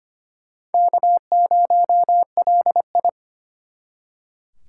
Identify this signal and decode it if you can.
Morse code, heard as a keyed tone, about 32 wpm, 700 Hz. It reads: K0LI